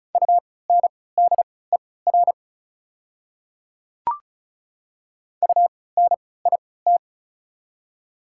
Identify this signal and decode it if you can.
Morse code, heard as a keyed tone, about 36 wpm, spaced as for 20 wpm, 690 Hz. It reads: UNDER UNIT
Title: 3x Character Spacing - Top 500 Words - 35wpm